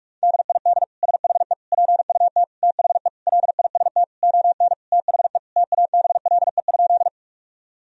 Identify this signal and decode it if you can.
Morse code, heard as a keyed tone, about 41 wpm, 700 Hz. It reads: DID SHE PUT THE LIST ON THE TABLE?
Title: Sentences from Top 1000 Words - Review Part 2 - 45wpm